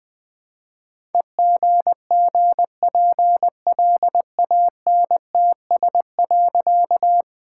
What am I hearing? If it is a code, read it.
Morse code, heard as a keyed tone, about 20 wpm, 700 Hz. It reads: EGGPLANTS.
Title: All-Age Appropriate Jokes - 20wpm